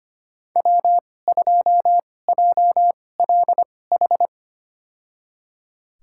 Morse code, heard as a keyed tone, about 25 wpm, spaced as for 18 wpm, 700 Hz. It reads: W2JLH
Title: USA Call Signs - 5 Characters - 2x Character Spacing - 25wpm